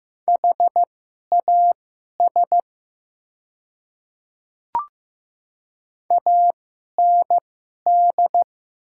Morse code, heard as a keyed tone, about 15 wpm, spaced as for 11 wpm, 700 Hz. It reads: HAS AND
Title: 2x Character Spacing - Top 100 Words - 15wpm